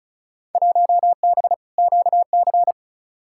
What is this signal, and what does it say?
Morse code, heard as a keyed tone, about 34 wpm, 700 Hz. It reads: 1B QC
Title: Field Day Exchanges Encore - 35wpm